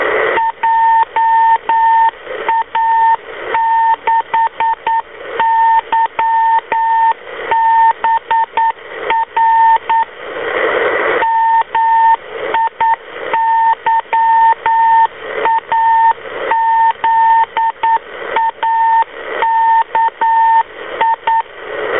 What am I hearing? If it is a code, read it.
Morse code, heard as a keyed tone, about 9 wpm, 910 Hz, with radio noise. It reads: JA6YBR MIYAZAKI